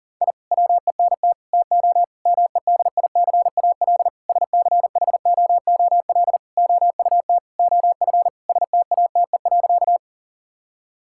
Morse code, heard as a keyed tone, about 40 wpm, 700 Hz. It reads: I WENT TO MEDICAL SCHOOL OUT OF STATE.
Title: Sentences from Top 1000 Words - Review Part 5 - 40wpm